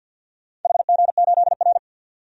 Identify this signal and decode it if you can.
Morse code, heard as a keyed tone, about 51 wpm, 700 Hz. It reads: HG8R
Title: DXCC-100 Call Signs - 4 Characters - 50wpm